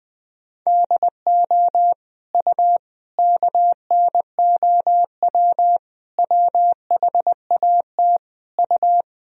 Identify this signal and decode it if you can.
Morse code, heard as a keyed tone, about 21 wpm, 700 Hz. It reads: DO U KNOW WHAT U
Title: Sentences from Top 100 Words and Abbreviations 20wpm